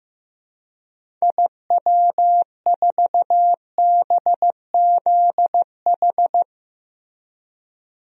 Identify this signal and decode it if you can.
Morse code, heard as a keyed tone, about 15 wpm, 700 Hz. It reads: IW4BZH
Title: DXCC-100 Call Signs - 6 Characters - 15wpm